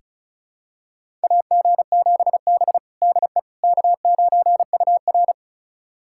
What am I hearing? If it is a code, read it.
Morse code, heard as a keyed tone, about 36 wpm, 700 Hz. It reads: AG7B DE K9UR
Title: USA Callsign DE Callsign - 4 Characters - 35wpm